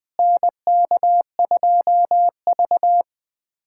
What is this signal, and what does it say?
Morse code, heard as a keyed tone, about 20 wpm, 700 Hz. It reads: NK2V
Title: USA Callsign DE Callsign - 4 Characters - 20wpm